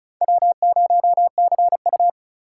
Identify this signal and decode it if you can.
Morse code, heard as a keyed tone, about 44 wpm, 700 Hz. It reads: W0CU